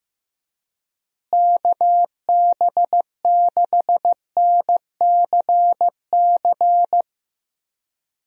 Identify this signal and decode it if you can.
Morse code, heard as a keyed tone, about 15 wpm, 700 Hz. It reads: KB6NCC
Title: USA Call Signs - 6 Characters - Repeated 2x - 15wpm